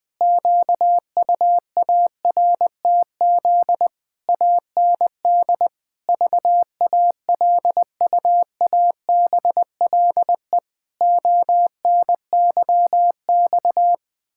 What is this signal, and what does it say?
Morse code, heard as a keyed tone, about 21 wpm, 700 Hz. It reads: QUARTZ AND VALUABLE ONYX
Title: Pangram Challenge v2 - 20wpm